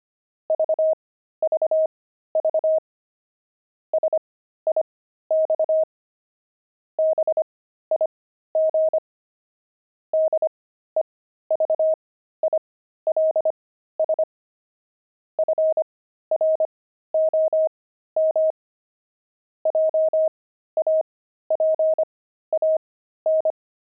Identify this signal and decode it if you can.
Morse code, heard as a keyed tone, about 25 wpm, spaced as for 13 wpm, 630 Hz. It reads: VVV SIX BIG DEVILS FROM JAPAN
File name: pangram-fs-13-cs-25-devils